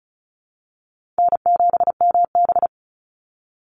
Morse code, heard as a keyed tone, about 34 wpm, 700 Hz. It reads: N7MB
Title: Easy Call Signs Repeated 3x - 35wpm